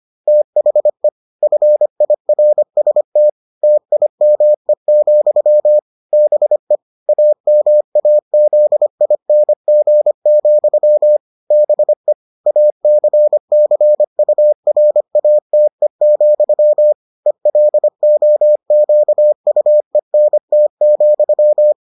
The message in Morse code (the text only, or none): THE FIRST TIME, BE AMAZING, BE ACCURATE, ELOQUENT,